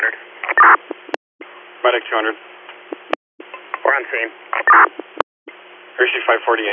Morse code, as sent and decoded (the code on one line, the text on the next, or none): none